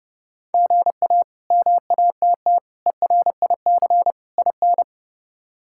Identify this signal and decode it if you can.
Morse code, heard as a keyed tone, about 30 wpm, 700 Hz. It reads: GA MATT ERIC IN
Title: K1USN SST QSO Practice - 30wpm